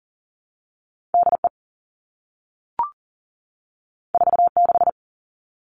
Morse code, heard as a keyed tone, about 40 wpm, 700 Hz. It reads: DE 46
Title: All Cut Numbers - 2-Digit 40wpm